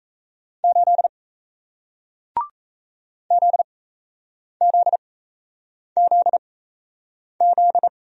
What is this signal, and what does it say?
Morse code, heard as a keyed tone, about 37 wpm, spaced as for 8 wpm, 700 Hz. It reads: 8 ZZZZ